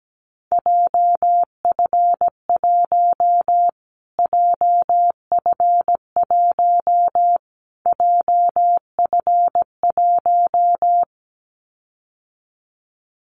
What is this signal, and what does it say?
Morse code, heard as a keyed tone, about 17 wpm, 700 Hz. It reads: JF1 JF1 JF1
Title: Callsign Prefixes for Top 25 DXCC - Repeated 3x - 17wpm